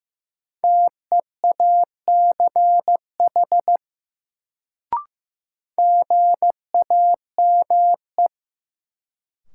Morse code, heard as a keyed tone, about 15 wpm, 700 Hz. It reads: TEACH GAME
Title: Sentences from Top 500 Words - New Word Drill - 15wpm